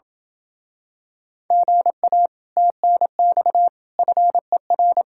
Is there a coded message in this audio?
Yes: Morse code from 1.5 to 5.0 s, heard as a keyed tone, about 26 wpm, 700 Hz. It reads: GA TNX FER